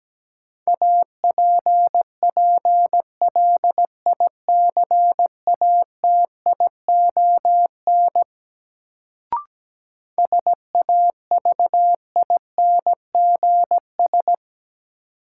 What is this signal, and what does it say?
Morse code, heard as a keyed tone, about 17 wpm, 700 Hz. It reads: APPLICATION SAVINGS
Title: Sentences from Top 1200 Words - New Word Drill - 17wpm